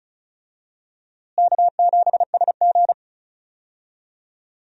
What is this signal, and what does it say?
Morse code, heard as a keyed tone, about 36 wpm, 700 Hz. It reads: K7SG
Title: USA Call Signs - 4 Characters - Repeated 3x - 35wpm